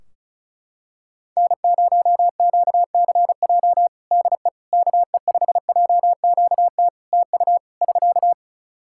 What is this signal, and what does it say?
Morse code, heard as a keyed tone, about 34 wpm, 700 Hz. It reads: N0QCJ DE KE5JQT TU <SK>
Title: Practice QSO - Exchange 4 - Ending - 35wpm